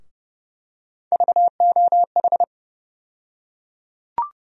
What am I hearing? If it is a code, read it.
Morse code, heard as a keyed tone, about 29 wpm, 700 Hz. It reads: VOH
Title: USA Call Sign Postfixes - 3 Letters 30wpm